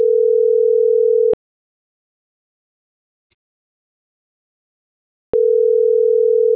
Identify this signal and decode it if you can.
Ringback tone (North America), 440 + 480 Hz.
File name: CENTRAL SAGOT / out isai.hernandez